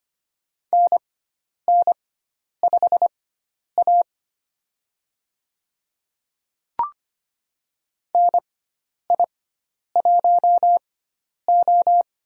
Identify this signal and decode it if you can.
Morse code, heard as a keyed tone, about 25 wpm, spaced as for 10 wpm, 700 Hz. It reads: NN5A NI1O
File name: Lesson 16g - Letter c - Call Signs 25wpm